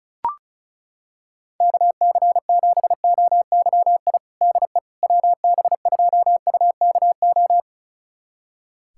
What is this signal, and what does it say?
Morse code, heard as a keyed tone, about 34 wpm, 700 Hz. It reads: KC7OYI DE WB2UKO
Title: USA Callsign DE Callsign - 6 Characters - 35wpm